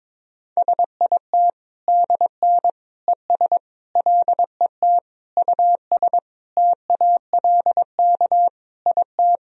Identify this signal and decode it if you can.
Morse code, heard as a keyed tone, about 22 wpm, 700 Hz. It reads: SIT DN ES LET US TALK IT
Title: Sentences from Top 500 Words and Abbreviations 22wpm